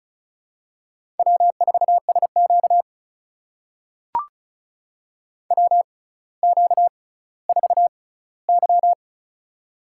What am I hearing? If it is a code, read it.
Morse code, heard as a keyed tone, about 36 wpm, spaced as for 18 wpm, 700 Hz. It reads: W4SQ WQ4Y
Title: USA Call Signs - 4 Characters - 6x Character Spacing - 35wpm